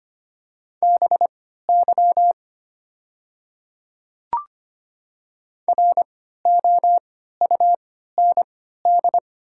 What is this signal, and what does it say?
Morse code, heard as a keyed tone, about 25 wpm, spaced as for 14 wpm, 700 Hz. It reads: BY ROUND
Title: 3x Character Spacing - Top 500 Words - 25wpm